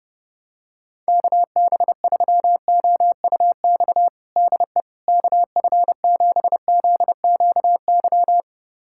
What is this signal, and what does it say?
Morse code, heard as a keyed tone, about 30 wpm, 700 Hz. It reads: KB3OUX DE KF7ZQY
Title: USA Callsign DE Callsign - 6 Characters - 30wpm